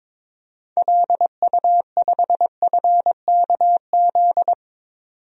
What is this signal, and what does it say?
Morse code, heard as a keyed tone, about 22 wpm, 700 Hz. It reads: LU5FKZ
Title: DXCC-100 Call Signs - 6 Characters - 22wpm